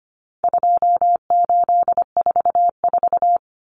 Morse code, heard as a keyed tone, about 25 wpm, 700 Hz. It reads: 2844
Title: Numbers - 4-Digit 25wpm